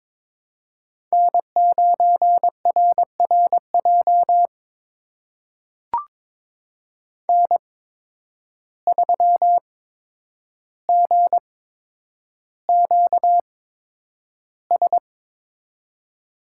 Morse code, heard as a keyed tone, about 22 wpm, spaced as for 9 wpm, 700 Hz. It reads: N9RRJ N3GQS
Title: USA Call Signs - 5 Characters - 8x Character Spacing - 22wpm